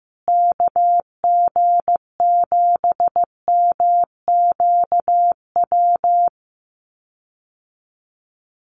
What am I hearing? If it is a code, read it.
Morse code, heard as a keyed tone, about 15 wpm, 700 Hz. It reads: KG7MQW
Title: USA Call Signs Repeated 2x 15wpm